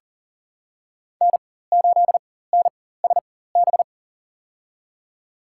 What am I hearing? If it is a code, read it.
Morse code, heard as a keyed tone, about 40 wpm, spaced as for 19 wpm, 700 Hz. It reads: N8NSB
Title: USA Call Signs - 5 Characters - 4x Character Spacing - 40wpm